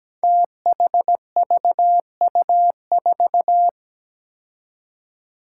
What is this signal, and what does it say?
Morse code, heard as a keyed tone, about 17 wpm, 700 Hz. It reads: THVU4
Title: Five Challenging Letter-Number Combinations 17wpm